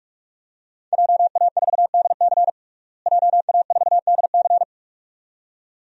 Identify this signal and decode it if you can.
Morse code, heard as a keyed tone, about 43 wpm, 700 Hz. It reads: JA4DC JA4DC
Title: DXCC-100 Call Signs - 5 Characters - Repeated 2x - 45wpm